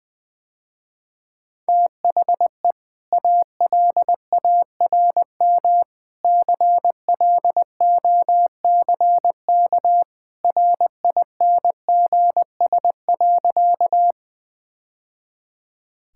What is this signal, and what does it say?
Morse code, heard as a keyed tone, about 20 wpm, 700 Hz. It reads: THE ALARM CLOCK RINGS.
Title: Easy Sentences with Hints - 20wpm